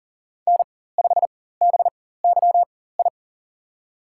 Morse code, heard as a keyed tone, about 40 wpm, spaced as for 19 wpm, 700 Hz. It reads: N5BYI
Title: USA Call Signs - 4x Character Spacing - 40wpm